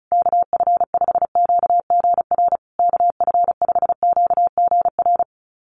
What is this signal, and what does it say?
Morse code, heard as a keyed tone, about 34 wpm, 700 Hz. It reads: KF5QGR KF5QGR
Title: USA Call Signs Repeated 3x 35wpm